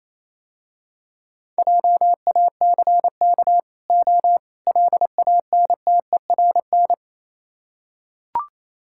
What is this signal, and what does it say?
Morse code, heard as a keyed tone, about 28 wpm, 700 Hz. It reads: JACK O LANTERN